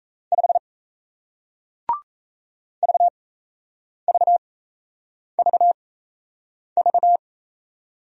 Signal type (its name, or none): Morse code